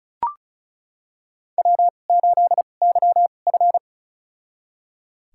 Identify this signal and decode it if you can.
Morse code, heard as a keyed tone, about 36 wpm, spaced as for 25 wpm, 700 Hz. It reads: W8YF
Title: USA Call Signs - 2x Character Spacing - 35wpm